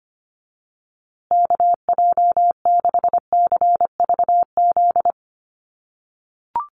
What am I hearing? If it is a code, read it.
Morse code, heard as a keyed tone, about 25 wpm, 700 Hz. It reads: KJ6CVZ
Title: USA Call Signs Repeated 2x 25wpm